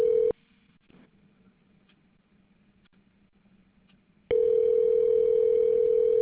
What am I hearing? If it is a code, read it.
Ringback tone (North America), 440 + 480 Hz.